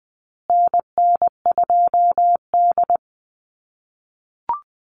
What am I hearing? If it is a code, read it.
Morse code, heard as a keyed tone, about 20 wpm, 700 Hz. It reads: NN2D
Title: USA Call Signs Repeated 2x 20wpm